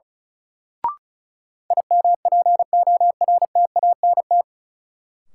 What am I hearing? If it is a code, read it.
Morse code, heard as a keyed tone, about 36 wpm, 700 Hz. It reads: IMPORTANT
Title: Sentences from Top 600 Words - New Word Drill - 35wpm